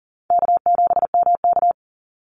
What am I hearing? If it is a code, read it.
Morse code, heard as a keyed tone, about 38 wpm, 700 Hz. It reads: K7MK
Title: NA SOTA Activator Callsigns 40wpm